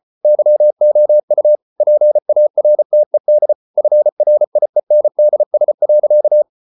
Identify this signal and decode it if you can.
Morse code, heard as a keyed tone, about 34 wpm, 600 Hz. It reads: YOU PARTED FRIENDS.